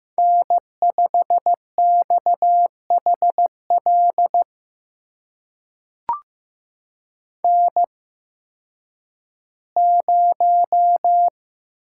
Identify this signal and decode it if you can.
Morse code, heard as a keyed tone, about 15 wpm, spaced as for 10 wpm, 700 Hz. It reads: N5XHL N0
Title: USA Call Signs - 5 Characters - 8x Character Spacing - 15wpm